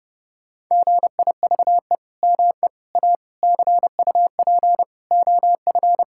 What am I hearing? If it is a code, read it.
Morse code, heard as a keyed tone, about 30 wpm, 700 Hz. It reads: GIVE ME A CUP OF
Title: Sentences from Top 1600 Words 30wpm